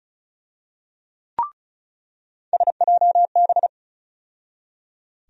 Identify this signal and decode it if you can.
Morse code, heard as a keyed tone, about 34 wpm, 700 Hz. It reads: SJB
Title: USA Call Sign Postfixes - 3 Letters 35wpm